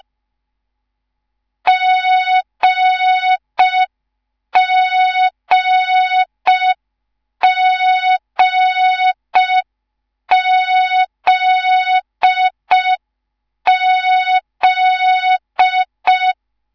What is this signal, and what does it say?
Morse code, heard as a keyed tone, about 5 wpm, 3700 Hz. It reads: GGGZZ